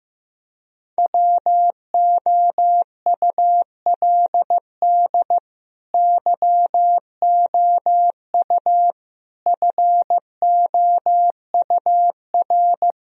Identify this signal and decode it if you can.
Morse code, heard as a keyed tone, about 15 wpm, 700 Hz. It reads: WOULD YOU FOUR
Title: Sentences from Top 1000 Words - Review Part 1 - 15wpm